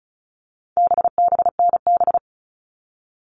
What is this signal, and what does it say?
Morse code, heard as a keyed tone, about 34 wpm, 700 Hz. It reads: BBNB
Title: All Cut Numbers - 4-Digit 35wpm